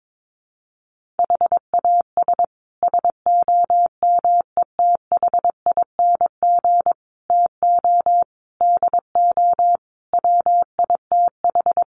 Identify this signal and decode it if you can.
Morse code, heard as a keyed tone, about 22 wpm, 700 Hz. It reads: HAS SOMETHING TO DO WITH
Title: Sets of 5 Words Encore 22wpm